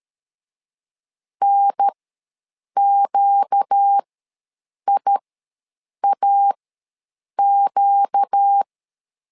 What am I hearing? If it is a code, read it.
Morse code, heard as a keyed tone, about 13 wpm, spaced as for 7 wpm, 800 Hz. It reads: NQIAQ
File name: LEZIONI / 4 Q ws 12wpm 1912